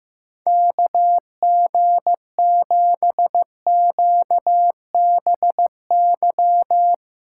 Morse code, heard as a keyed tone, about 15 wpm, 700 Hz. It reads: KG7QBY
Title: USA Callsign DE Callsign - 6 Characters - 15wpm